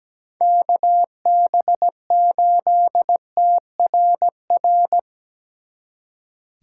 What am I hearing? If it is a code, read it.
Morse code, heard as a keyed tone, about 20 wpm, 700 Hz. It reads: KB8TRR
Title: Speed Racing - USA 2x3 Call Signs 17wpm